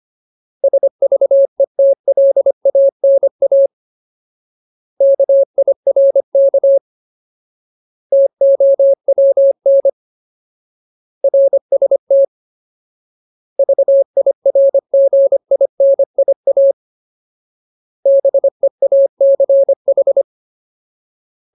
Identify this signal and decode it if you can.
Morse code, heard as a keyed tone, about 25 wpm, 550 Hz. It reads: SVETLANA KIRK TOWN RST VIRGINIA BEACH
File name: Ultimate Word List / 25WPM Ultimate